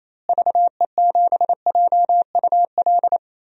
Morse code, heard as a keyed tone, about 28 wpm, 700 Hz. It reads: VE7JUL
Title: DXCC-100 Call Signs - 6 Characters - Repeated 3x - 28wpm